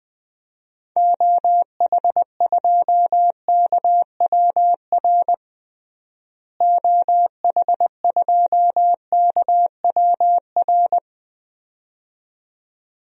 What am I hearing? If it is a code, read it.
Morse code, heard as a keyed tone, about 20 wpm, 700 Hz. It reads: OH2KWR OH2KWR
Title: DXCC-100 Call Signs - 6 Characters - Repeated 2x - 20wpm